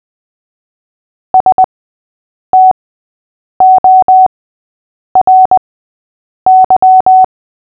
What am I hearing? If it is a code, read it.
Morse code, heard as a keyed tone, about 20 wpm, spaced as for 8 wpm, 750 Hz. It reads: STORY
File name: cw / fists / Short-Story-8-8wpm